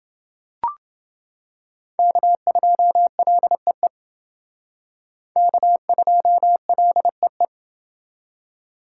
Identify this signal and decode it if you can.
Morse code, heard as a keyed tone, about 28 wpm, 700 Hz. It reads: K2LEE K2LEE